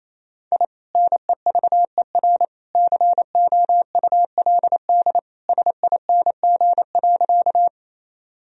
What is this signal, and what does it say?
Morse code, heard as a keyed tone, about 28 wpm, 700 Hz. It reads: I NEVER COULD SING.
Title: Sentences from Top 1000 Words - Review Part 2 - 28wpm